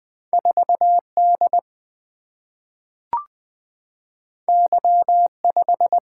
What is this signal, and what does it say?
Morse code, heard as a keyed tone, about 20 wpm, 700 Hz. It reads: 4D Y5
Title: Single-Letter Single-Number 20wpm